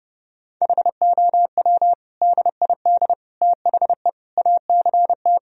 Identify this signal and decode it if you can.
Morse code, heard as a keyed tone, about 30 wpm, 700 Hz. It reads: HOW DID THE ACT
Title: Sentences from Top 1000 Words - Review Part 4 - 30wpm